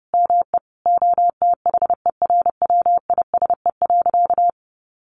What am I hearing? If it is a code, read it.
Morse code, heard as a keyed tone, about 30 wpm, 700 Hz. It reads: ME OTHERWISE.
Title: Sentences from Top 1200 Words 30wpm